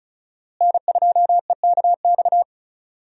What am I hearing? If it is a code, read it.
Morse code, heard as a keyed tone, about 36 wpm, 700 Hz. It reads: N2EKX
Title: USA Call Signs - 5 Characters - Repeated 2x - 35wpm